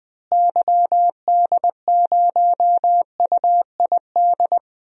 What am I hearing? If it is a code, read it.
Morse code, heard as a keyed tone, about 20 wpm, 700 Hz. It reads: YD0UID
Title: DXCC-100 Call Signs - 6 Characters - Repeated 3x - 20wpm